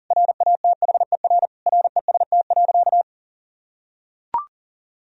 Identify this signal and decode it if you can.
Morse code, heard as a keyed tone, about 40 wpm, 700 Hz. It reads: RATHER REST.
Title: Sentences from Top 1000 Words - Review Part 4 - 40wpm